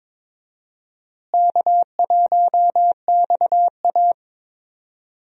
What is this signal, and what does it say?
Morse code, heard as a keyed tone, about 22 wpm, 700 Hz. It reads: K1XA
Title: USA Call Signs - 4 Characters - Repeated 2x - 22wpm